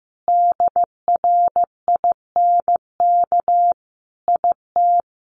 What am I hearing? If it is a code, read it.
Morse code, heard as a keyed tone, about 15 wpm, 700 Hz. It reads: DRINK IT
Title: Additional Easy Phrases 15wpm